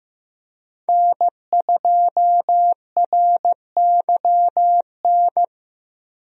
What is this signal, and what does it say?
Morse code, heard as a keyed tone, about 16 wpm, 700 Hz. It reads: N2RYN
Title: USA Call Signs - 5 Characters - Repeated 3x - 15wpm